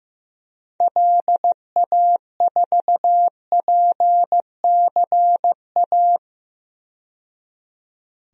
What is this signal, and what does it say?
Morse code, heard as a keyed tone, about 15 wpm, 700 Hz. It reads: LA4PCA
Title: DXCC-100 Call Signs - 6 Characters - 15wpm